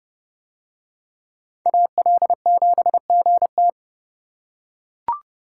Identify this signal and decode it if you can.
Morse code, heard as a keyed tone, about 29 wpm, 700 Hz. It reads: AL7GT